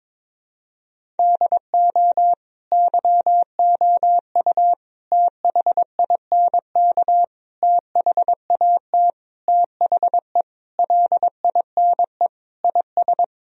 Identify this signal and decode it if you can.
Morse code, heard as a keyed tone, about 22 wpm, 700 Hz. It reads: DO YOU THINK THAT THE LINE IS
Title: Sentences from Top 1000 Words - Review Part 1 - 22wpm